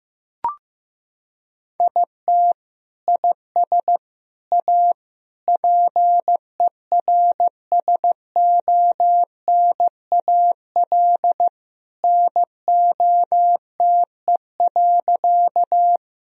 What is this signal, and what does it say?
Morse code, heard as a keyed tone, about 15 wpm, 700 Hz. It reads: IT IS A PERSONAL NOTE.
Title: Sentences from Top 1000 Words - Review Part 4 - 15wpm